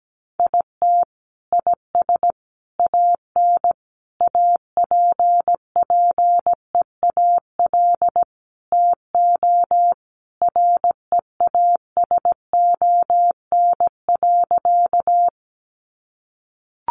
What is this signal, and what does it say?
Morse code, heard as a keyed tone, about 17 wpm, 700 Hz. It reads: IT IS AN APPEAL TO REASON.
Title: Sentences from Top 1300 Words 17wpm